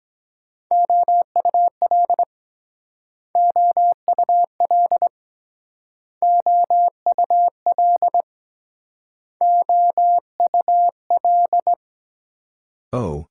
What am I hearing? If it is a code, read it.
Morse code, heard as a keyed tone, about 22 wpm, 700 Hz. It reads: OUL OUL OUL OUL